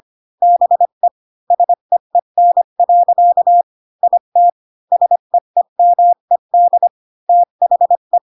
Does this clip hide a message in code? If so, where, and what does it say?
Yes: Morse code from 0.4 to 8.2 s, heard as a keyed tone, about 25 wpm, 700 Hz. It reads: BE SEEN. IT SEEMED THE